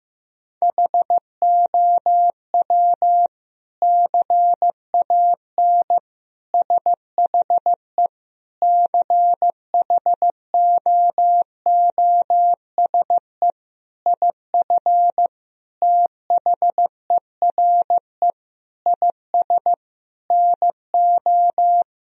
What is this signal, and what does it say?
Morse code, heard as a keyed tone, about 15 wpm, 700 Hz. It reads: HOW CAN SHE CHOOSE IF THERE IS NO
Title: Sentences from Top 1000 Words - Review Part 4 - 15wpm